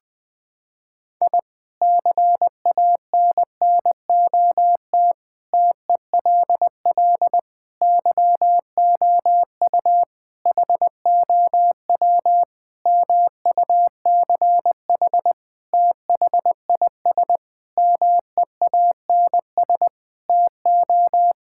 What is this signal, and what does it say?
Morse code, heard as a keyed tone, about 21 wpm, 700 Hz. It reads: I CANNOT TELL YOU HOW MUCH THIS MEANS TO
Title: Sentences with 5-Word Phrases - 20wpm